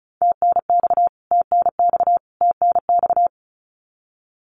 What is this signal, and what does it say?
Morse code, heard as a keyed tone, about 34 wpm, 700 Hz. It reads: TNX TNX TNX
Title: Abbreviations Repeated 3x - 35wpm